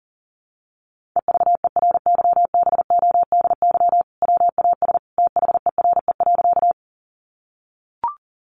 Morse code, heard as a keyed tone, about 40 wpm, 700 Hz. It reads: EVERYBODY WAS THERE.
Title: Sentences from Top 1400 Words 40wpm